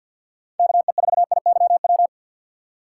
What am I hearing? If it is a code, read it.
Morse code, heard as a keyed tone, about 53 wpm, 690 Hz. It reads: KE4IYW
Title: USA Call Signs - 6 Characters - 8x Character Spacing - 50wpm